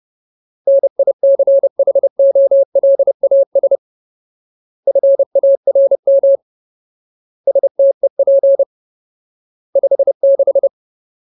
Morse code, heard as a keyed tone, about 30 wpm, 550 Hz. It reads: NICHOLAS FARM STEP 56
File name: Ultimate Word List / 30WPM Ultimate